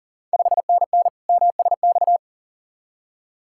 Morse code, heard as a keyed tone, about 40 wpm, 700 Hz. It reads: 5NN MSX
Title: Ontario QSO Party Exchange 40wpm